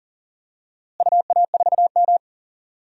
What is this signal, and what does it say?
Morse code, heard as a keyed tone, about 40 wpm, 700 Hz. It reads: UA4M
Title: DXCC-100 Call Signs - 4 Characters - Repeated 2x - 40wpm